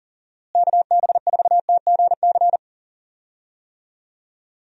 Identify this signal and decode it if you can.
Morse code, heard as a keyed tone, about 40 wpm, 700 Hz. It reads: KB4TGC
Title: USA Call Signs - 6 Characters - Repeated 2x - 40wpm